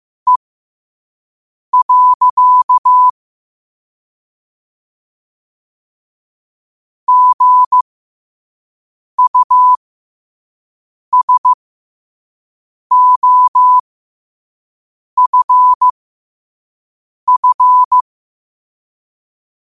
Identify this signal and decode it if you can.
Morse code, heard as a keyed tone, about 15 wpm, spaced as for 5 wpm, 1000 Hz. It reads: E. GUSOFF
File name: qotdfiles / 5wpm